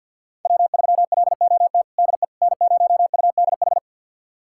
Morse code, heard as a keyed tone, about 46 wpm, 700 Hz. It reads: W3LOT DE N0UDH
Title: USA Callsign DE Callsign - 5 Characters - 50wpm